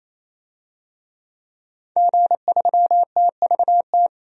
Morse code, heard as a keyed tone, about 28 wpm, 700 Hz. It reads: G3TVT
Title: DXCC-100 Call Signs - 5 Characters - 28wpm